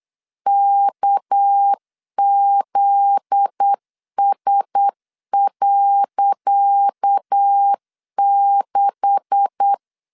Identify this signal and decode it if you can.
Morse code, heard as a keyed tone, about 8 wpm, 800 Hz. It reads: KZS.6
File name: LEZIONI / 10 SIGNS / 10C DOT 8wpm 3649